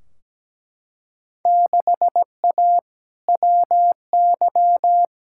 Morse code, heard as a keyed tone, about 17 wpm, 700 Hz. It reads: 6A WY
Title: Field Day Exchanges Encore - 17wpm